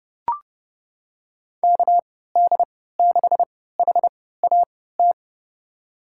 Morse code, heard as a keyed tone, about 32 wpm, spaced as for 17 wpm, 700 Hz. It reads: KD6HAT
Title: USA Call Signs - 3x Character Spacing - 30wpm